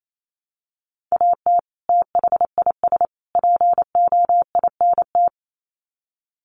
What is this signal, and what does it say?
Morse code, heard as a keyed tone, about 28 wpm, 700 Hz. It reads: AT THIS POINT
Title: Sets of 3 Words 28wpm